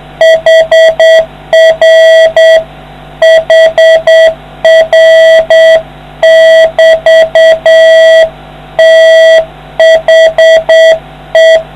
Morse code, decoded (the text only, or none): HR HR <BT> THE